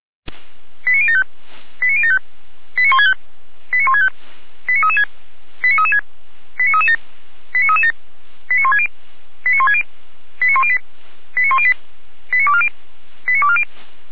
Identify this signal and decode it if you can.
Five-tone selective call (ZVEI), heard as quick runs of five tones, tones of about 70 ms. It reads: 89075 89075 89165 89165 89336 89336 89337 89337 89160 89160 89108 89108 89240 89240